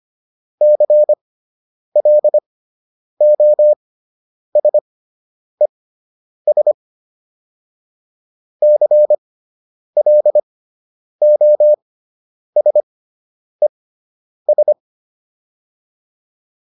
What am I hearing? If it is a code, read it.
Morse code, heard as a keyed tone, about 25 wpm, spaced as for 9 wpm, 600 Hz. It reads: CLOSES CLOSES
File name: ES6 09